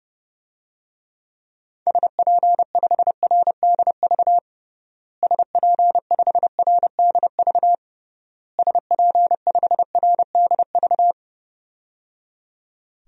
Morse code, heard as a keyed tone, about 30 wpm, 700 Hz. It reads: SP5RDV SP5RDV SP5RDV
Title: DXCC-100 Call Signs - 6 Characters - Repeated 3x - 30wpm